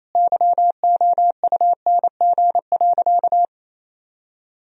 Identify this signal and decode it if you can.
Morse code, heard as a keyed tone, about 27 wpm, 700 Hz. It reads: YOUNG.
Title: Sentences from Top 400 Words and Abbreviations 28wpm